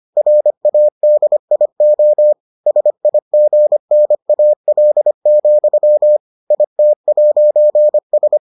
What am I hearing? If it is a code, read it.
Morse code, heard as a keyed tone, about 25 wpm, 600 Hz. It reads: RADIO SIGNAL, IT'S